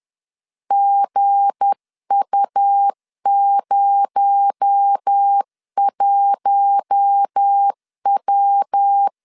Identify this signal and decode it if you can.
Morse code, heard as a keyed tone, about 10 wpm, 800 Hz. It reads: GU01W
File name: LEZIONI / 7 GF9Z / 7 G 10wpm 2501